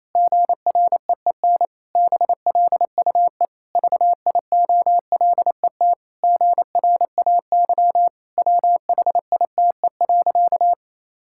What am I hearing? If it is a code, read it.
Morse code, heard as a keyed tone, about 28 wpm, 700 Hz. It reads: GREEN BLUE VIOLET GRAY WHITE.
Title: Pangram Challenge v2 - 28wpm